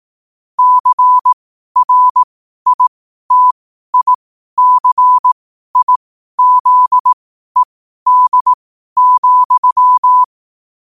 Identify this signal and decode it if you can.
Morse code, heard as a keyed tone, about 18 wpm, spaced as for 12 wpm, 1000 Hz. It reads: CRITICIZED,
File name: qotdfiles / 12wpm